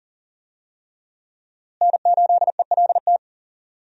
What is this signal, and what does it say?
Morse code, heard as a keyed tone, about 40 wpm, 700 Hz. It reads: N8ELT
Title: USA Call Signs - 5 Characters - Repeated 3x - 40wpm